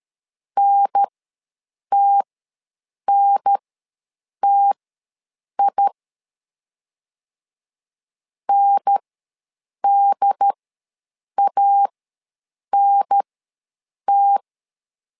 Morse code, heard as a keyed tone, about 13 wpm, spaced as for 7 wpm, 800 Hz. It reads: NTNTI NDANT
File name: LEZIONI / 2 UHDN / 2 UHDN ws 12wpm 1708